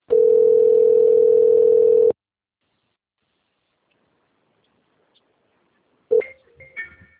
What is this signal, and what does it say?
Ringback tone (North America), 440 + 480 Hz.
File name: rg-1600-unknown- - - .7588